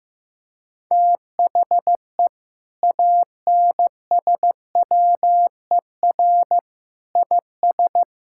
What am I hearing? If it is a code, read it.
Morse code, heard as a keyed tone, about 15 wpm, 700 Hz. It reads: THE ANSWER IS
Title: Sentences from Top 1600 Words 15wpm